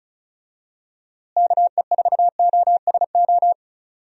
Morse code, heard as a keyed tone, about 36 wpm, 700 Hz. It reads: KE4OSO
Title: USA Call Signs - 6 Characters - Repeated 3x - 35wpm